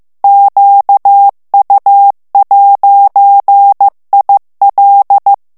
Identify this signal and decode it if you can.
Morse code, heard as a keyed tone, about 15 wpm, 800 Hz. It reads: QU'IL
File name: morse mp3 / FRANCAIS TEXTE CLAIR 2 VITESSE 14 MOTS MINUTES